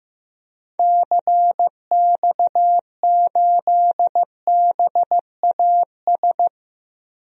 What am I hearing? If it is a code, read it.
Morse code, heard as a keyed tone, about 15 wpm, 700 Hz. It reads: CX8BAS
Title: DXCC-100 Call Signs - 6 Characters - 15wpm